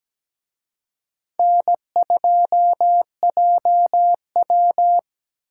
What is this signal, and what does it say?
Morse code, heard as a keyed tone, about 20 wpm, 700 Hz. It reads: N2JW
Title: Speed Racing - USA 1x2 Call Signs 17wpm